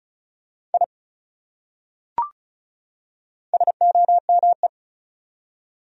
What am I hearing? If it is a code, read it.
Morse code, heard as a keyed tone, about 36 wpm, 700 Hz. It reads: I SOME
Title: Top 100 Words 35wpm